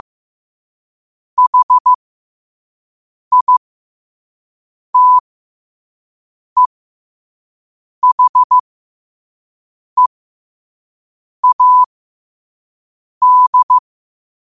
Morse code, heard as a keyed tone, about 15 wpm, spaced as for 5 wpm, 1000 Hz. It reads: HITEHEAD
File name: qotdfiles / 5wpm